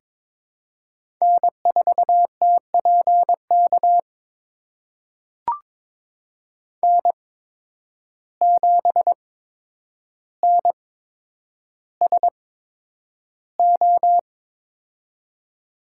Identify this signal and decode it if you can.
Morse code, heard as a keyed tone, about 22 wpm, spaced as for 9 wpm, 700 Hz. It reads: N4TPK N7NSO
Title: USA Call Signs - 5 Characters - 8x Character Spacing - 22wpm